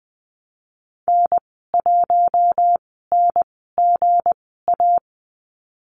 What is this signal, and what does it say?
Morse code, heard as a keyed tone, about 20 wpm, spaced as for 15 wpm, 700 Hz. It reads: N1NGA
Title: 186 Lesson 30g - Letter b - Call Signs 20wpm